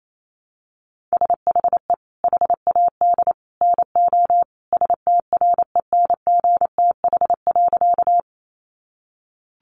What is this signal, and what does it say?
Morse code, heard as a keyed tone, about 28 wpm, 700 Hz. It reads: SHE HAD NO STRENGTH.
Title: Sentences from Top 1300 Words 28wpm